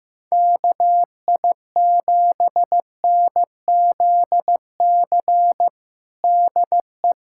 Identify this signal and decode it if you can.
Morse code, heard as a keyed tone, about 15 wpm, 700 Hz. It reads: KI7NZC DE
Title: USA Callsign DE Callsign - 6 Characters - 15wpm